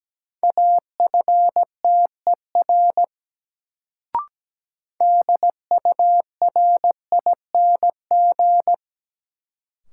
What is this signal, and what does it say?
Morse code, heard as a keyed tone, about 17 wpm, 700 Hz. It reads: AFTER DURING
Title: Sets of 2 Words - New Word Drill - 17wpm